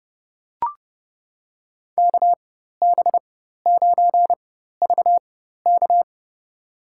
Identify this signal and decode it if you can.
Morse code, heard as a keyed tone, about 30 wpm, spaced as for 14 wpm, 700 Hz. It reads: KB9VK
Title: USA Call Signs - 5 Characters - 4x Character Spacing - 30wpm